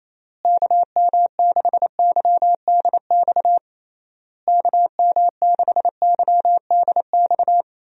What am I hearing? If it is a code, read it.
Morse code, heard as a keyed tone, about 28 wpm, 700 Hz. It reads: KM6YDX KM6YDX
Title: USA Call Signs - 6 Characters - Repeated 3x - 28wpm